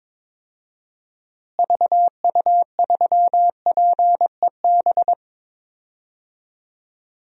Morse code, heard as a keyed tone, about 21 wpm, 700 Hz. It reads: VU3PEB
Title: DXCC-100 Call Signs - 6 Characters - Repeated 3x - 22wpm